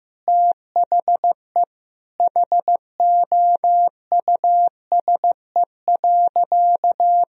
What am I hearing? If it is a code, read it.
Morse code, heard as a keyed tone, about 15 wpm, 700 Hz. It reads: THE HOUSE.
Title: Sentences from Top 400 Words and Abbreviations 15wpm